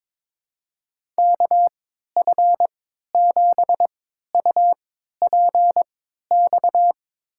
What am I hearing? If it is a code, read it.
Morse code, heard as a keyed tone, about 22 wpm, spaced as for 13 wpm, 700 Hz. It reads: KF7UPX
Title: USA Call Signs - 6 Characters - 3x Character Spacing - 22wpm